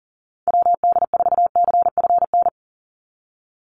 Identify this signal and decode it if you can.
Morse code, heard as a keyed tone, about 40 wpm, 700 Hz. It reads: WD4CFN
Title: NA SOTA Chaser Callsigns 40wpm